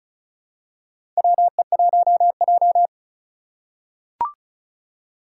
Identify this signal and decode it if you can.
Morse code, heard as a keyed tone, about 36 wpm, 700 Hz. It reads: WE1J
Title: USA Call Signs - 4 Characters - 2x Character Spacing - 35wpm